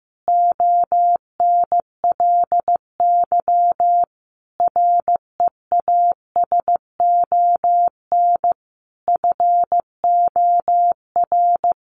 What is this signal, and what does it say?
Morse code, heard as a keyed tone, about 15 wpm, 700 Hz. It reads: ONLY REASON FOR
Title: Sentences from Top 700 Words 15wpm